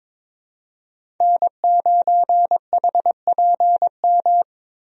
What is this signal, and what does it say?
Morse code, heard as a keyed tone, about 22 wpm, 700 Hz. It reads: N9HPM
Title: USA Call Signs - 5 Characters - Repeated 3x - 22wpm